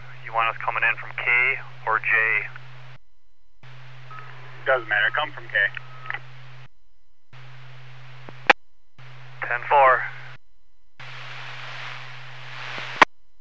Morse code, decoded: none

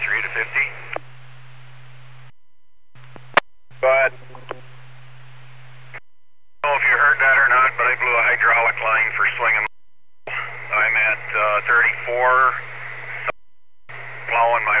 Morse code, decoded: none

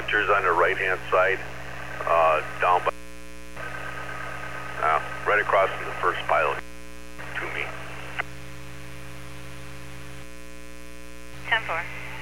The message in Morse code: none

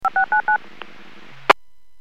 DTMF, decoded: 5699

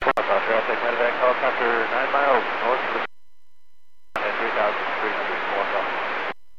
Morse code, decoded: none

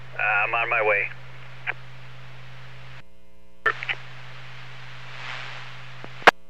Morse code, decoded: none

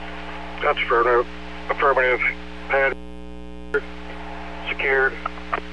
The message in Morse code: none

